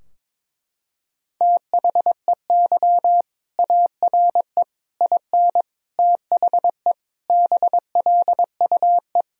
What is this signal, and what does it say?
Morse code, heard as a keyed tone, about 22 wpm, 700 Hz. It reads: THEY ARE IN THE BLUE
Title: Sentences from Top 1000 Words - Review Part 3 - 22wpm